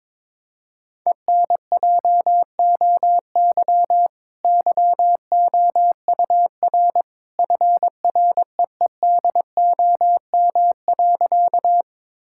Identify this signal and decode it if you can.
Morse code, heard as a keyed tone, about 22 wpm, 700 Hz. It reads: ENJOY YOUR FREEDOM.
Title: Sentences from Top 1000 Words - Review Part 5 - 22wpm